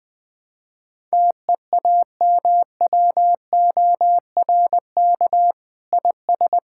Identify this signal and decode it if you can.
Morse code, heard as a keyed tone, about 20 wpm, 700 Hz. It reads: TEAMWORK IS
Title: Sentences with 7-Word Phrases - 20wpm